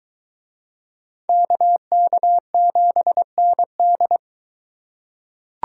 Morse code, heard as a keyed tone, about 20 wpm, 700 Hz. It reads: KK7ND